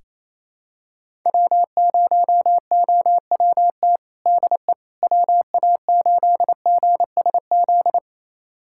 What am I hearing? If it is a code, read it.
Morse code, heard as a keyed tone, about 27 wpm, 700 Hz. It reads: W0OWT DE WA8GSZ
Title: USA Callsign DE Callsign 28wpm